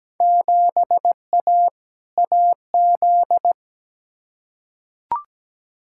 Morse code, heard as a keyed tone, about 17 wpm, 700 Hz. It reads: 7A AZ
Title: Field Day Exchanges - 17wpm